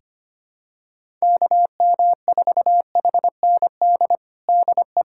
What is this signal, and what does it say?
Morse code, heard as a keyed tone, about 25 wpm, 700 Hz. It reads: KM4HND DE
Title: USA Callsign DE Callsign 25wpm